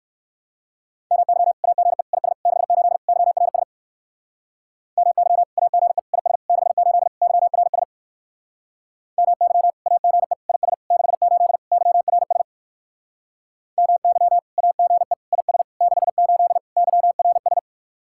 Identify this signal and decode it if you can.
Morse code, heard as a keyed tone, about 66 wpm, 700 Hz. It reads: MY AGE IS 68 YRS MY AGE IS 68 YRS MY AGE IS 68 YRS MY AGE IS 68 YRS